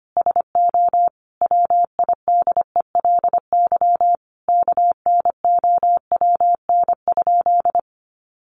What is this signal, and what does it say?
Morse code, heard as a keyed tone, about 25 wpm, 700 Hz. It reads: SO WIDELY KNOWN?
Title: Sentences from Top 1200 Words 25wpm